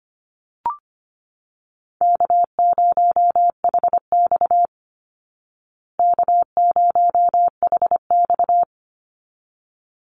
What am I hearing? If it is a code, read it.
Morse code, heard as a keyed tone, about 25 wpm, 700 Hz. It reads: K0HX K0HX
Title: Easy Call Signs Repeated 2x - 25wpm